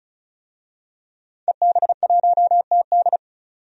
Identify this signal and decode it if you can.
Morse code, heard as a keyed tone, about 34 wpm, 700 Hz. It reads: EB1TD
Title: DXCC-100 Call Signs - 5 Characters - 35wpm